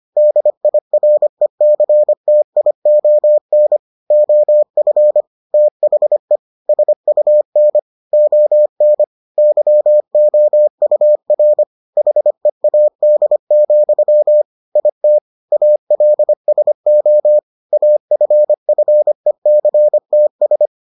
Morse code, heard as a keyed tone, about 25 wpm, 600 Hz. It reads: DIRECTION OF THE SUN ON YOUR HEAD, IT ALSO AFFECTS